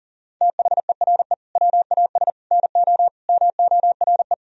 Morse code, heard as a keyed tone, about 40 wpm, 700 Hz. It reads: THERE WAS NO MORE
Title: Sentences from Top 1000 Words - Review Part 4 - 40wpm